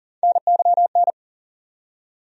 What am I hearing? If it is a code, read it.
Morse code, heard as a keyed tone, about 40 wpm, 700 Hz. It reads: NYN
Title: USA Call Sign Postfixes - 3 Letters 40wpm